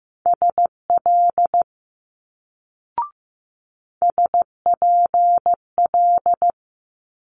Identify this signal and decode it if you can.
Morse code, heard as a keyed tone, about 15 wpm, 700 Hz. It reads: SL SPL
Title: Multi-letter Phonemes Quick Fire 15wpm